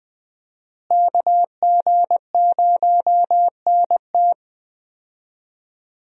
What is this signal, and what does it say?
Morse code, heard as a keyed tone, about 20 wpm, 700 Hz. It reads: KG0NT
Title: USA Call Signs - 5 Characters - Repeated 2x - 20wpm